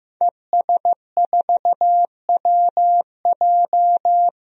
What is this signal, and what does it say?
Morse code, heard as a keyed tone, about 16 wpm, 700 Hz. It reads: ES4WJ
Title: DXCC-100 Call Signs - 5 Characters - 15wpm